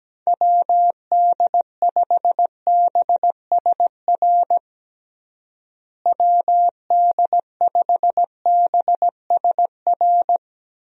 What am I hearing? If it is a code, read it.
Morse code, heard as a keyed tone, about 17 wpm, 700 Hz. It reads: WD5BSR WD5BSR
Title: USA Call Signs - 6 Characters - Repeated 3x - 17wpm